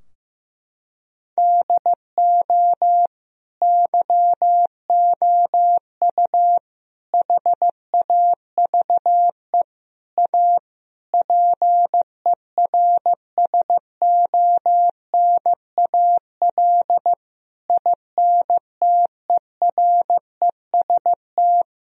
Morse code, heard as a keyed tone, about 16 wpm, 700 Hz. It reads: DO YOU HAVE A PERSONAL INTEREST
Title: Sentences from Top 1000 Words - Review Part 4 - 15wpm